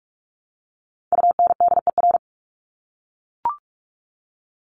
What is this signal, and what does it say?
Morse code, heard as a keyed tone, about 44 wpm, 700 Hz. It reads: UNDER
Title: Top 200 Words 45wpm